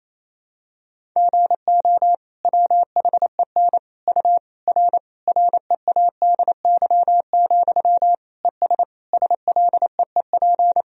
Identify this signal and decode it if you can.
Morse code, heard as a keyed tone, about 28 wpm, 700 Hz. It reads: GO WHEN U R READY, ES SLEEP
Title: Sentences from Top 500 Words and Abbreviations 28wpm